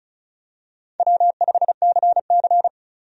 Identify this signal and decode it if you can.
Morse code, heard as a keyed tone, about 36 wpm, 700 Hz. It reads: W5CC
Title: USA Call Signs - 4 Characters - 35wpm